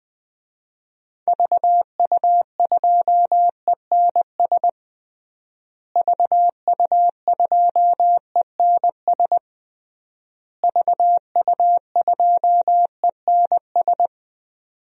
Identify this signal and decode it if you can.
Morse code, heard as a keyed tone, about 20 wpm, 700 Hz. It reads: VU2ENS VU2ENS VU2ENS
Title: DXCC-100 Call Signs - 6 Characters - Repeated 3x - 20wpm